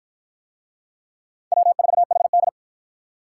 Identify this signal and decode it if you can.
Morse code, heard as a keyed tone, about 49 wpm, 700 Hz. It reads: W4HD